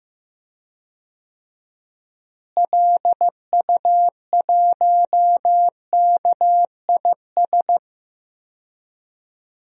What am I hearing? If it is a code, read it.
Morse code, heard as a keyed tone, about 15 wpm, 700 Hz. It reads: LU1KIS
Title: DXCC-100 Call Signs - 6 Characters - Repeated 2x - 15wpm